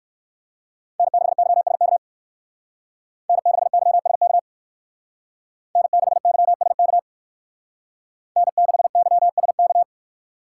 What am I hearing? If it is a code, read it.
Morse code, heard as a keyed tone, about 62 wpm, 700 Hz. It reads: N6YSK N6YSK N6YSK N6YSK